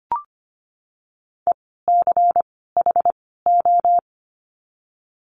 Morse code, heard as a keyed tone, about 25 wpm, 700 Hz. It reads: E C H O
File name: course / 25:10wpm / 094 Lesson 16f - Letter c - Words 25wpm